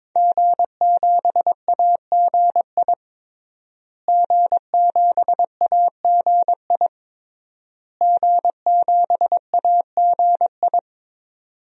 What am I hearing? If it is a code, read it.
Morse code, heard as a keyed tone, about 22 wpm, 700 Hz. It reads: G7AGI G7AGI G7AGI
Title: DXCC-100 Call Signs - 5 Characters - Repeated 3x - 22wpm